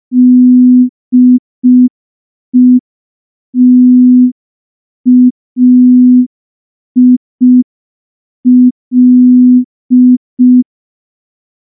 Morse code, heard as a keyed tone, about 5 wpm, 250 Hz. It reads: DETAIL
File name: morse / morse-training-05WPM